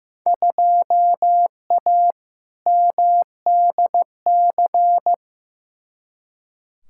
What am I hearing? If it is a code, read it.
Morse code, heard as a keyed tone, about 15 wpm, 700 Hz. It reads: 2A MDC
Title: Field Day Exchanges - 15wpm